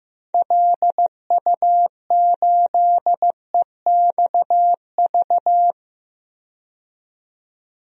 Morse code, heard as a keyed tone, about 16 wpm, 700 Hz. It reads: LU8EXV
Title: DXCC-100 Call Signs - 6 Characters - 15wpm